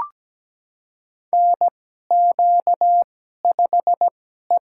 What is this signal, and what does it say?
Morse code, heard as a keyed tone, about 17 wpm, spaced as for 12 wpm, 700 Hz. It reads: NQ5E
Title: USA Call Signs - 4 Characters - 2x Character Spacing - 17wpm